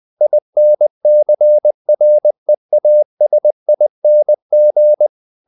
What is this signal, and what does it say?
Morse code, heard as a keyed tone, about 20 wpm, 600 Hz. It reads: INCREASING
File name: the old man and the sea part 2 20wpm